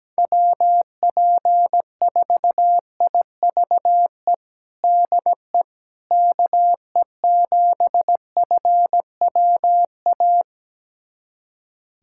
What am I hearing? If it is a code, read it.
Morse code, heard as a keyed tone, about 17 wpm, 700 Hz. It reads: WP4IVE DE KE7FWA
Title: USA Callsign DE Callsign 17wpm